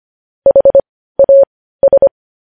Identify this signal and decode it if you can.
Morse code, heard as a keyed tone, about 25 wpm, spaced as for 15 wpm, 550 Hz. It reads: HAS
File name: podcast / News headlines-1x-25x15